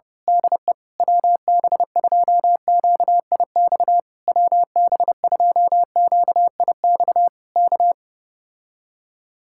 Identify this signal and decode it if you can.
Morse code, heard as a keyed tone, about 29 wpm, 700 Hz. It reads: DE WB2QIX WB2QIX K
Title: Speed-Racing Common QSO Phrases Encore - 25wpm